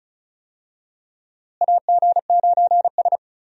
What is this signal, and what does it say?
Morse code, heard as a keyed tone, about 36 wpm, 700 Hz. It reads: AG9S
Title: USA Call Signs - 4 Characters - Repeated 2x - 35wpm